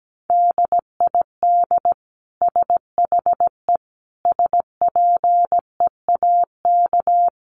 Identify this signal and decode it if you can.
Morse code, heard as a keyed tone, about 17 wpm, 700 Hz. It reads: DID SHE SPEAK
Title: Sentences from Top 1100 Words 17wpm